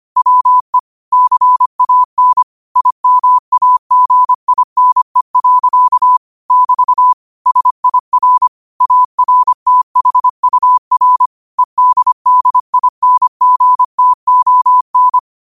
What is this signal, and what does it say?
Morse code, heard as a keyed tone, about 25 wpm, 1000 Hz. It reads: WE CAN IMAGINE. <BT> SIR ARTHUR EDDINGTON
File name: qotdfiles / 25wpm